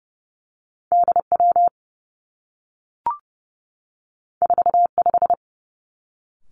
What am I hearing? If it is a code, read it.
Morse code, heard as a keyed tone, about 30 wpm, 700 Hz. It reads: D W 4 5
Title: Two Challenging Letter-Number Combinations 30wpm